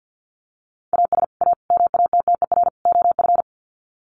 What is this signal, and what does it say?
Morse code, heard as a keyed tone, about 50 wpm, 700 Hz. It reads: AS A MATTER OF
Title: Sets of 4 Words 50wpm